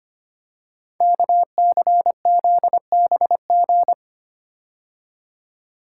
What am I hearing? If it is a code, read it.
Morse code, heard as a keyed tone, about 25 wpm, 700 Hz. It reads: KCZBG
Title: Code Groups - 5 Letters - 25wpm